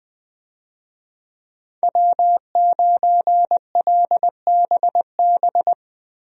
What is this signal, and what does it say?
Morse code, heard as a keyed tone, about 20 wpm, 700 Hz. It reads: W9LBB
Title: USA Call Signs - 5 Characters - Repeated 3x - 20wpm